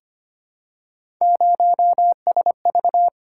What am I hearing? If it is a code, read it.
Morse code, heard as a keyed tone, about 25 wpm, 700 Hz. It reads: 0SV
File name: Three Letter-Number Combinations - 25wpm